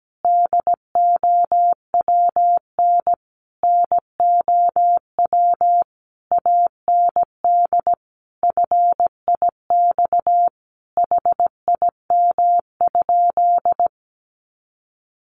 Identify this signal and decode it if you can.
Morse code, heard as a keyed tone, about 17 wpm, 700 Hz. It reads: DOWN NOW AND FIX HIM?
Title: Sentences from Top 1400 Words 17wpm